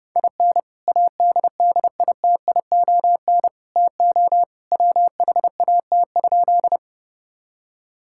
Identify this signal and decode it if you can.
Morse code, heard as a keyed tone, about 32 wpm, 700 Hz. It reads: IN ADDITION TO WHAT?
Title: Sentences from Top 1000 Words - Review Part 6 - 30wpm